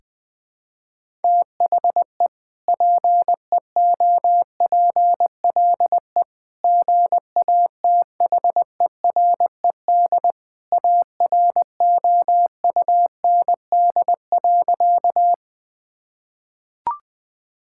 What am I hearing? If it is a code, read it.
Morse code, heard as a keyed tone, about 21 wpm, 700 Hz. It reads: THE PEOPLE GATHERED AROUND.
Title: Sentences with 2-Word Phrases - 20wpm